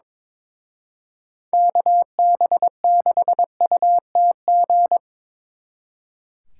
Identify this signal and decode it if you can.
Morse code, heard as a keyed tone, about 26 wpm, 700 Hz. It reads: KB6UTG
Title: Speed Racing - USA 2x3 Call Signs 22wpm